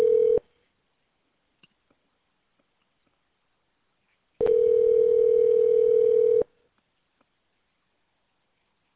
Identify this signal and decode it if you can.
Ringback tone (North America), 440 + 480 Hz.